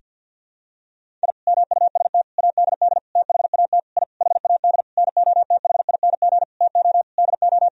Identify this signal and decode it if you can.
Morse code, heard as a keyed tone, about 52 wpm, 700 Hz. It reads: I MUST ADD THAT I HAD NOTHING TO DO
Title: Sentences from Top 1000 Words - Review Part 1 - 50wpm